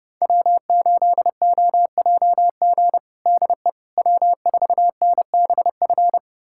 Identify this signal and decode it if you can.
Morse code, heard as a keyed tone, about 30 wpm, 700 Hz. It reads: W8OJG DE W4NBF
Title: USA Callsign DE Callsign - 5 Characters - 30wpm